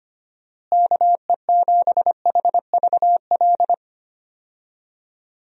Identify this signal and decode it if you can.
Morse code, heard as a keyed tone, about 25 wpm, 700 Hz. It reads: KE7HVL
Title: USA Call Signs - 6 Characters - Repeated 2x - 25wpm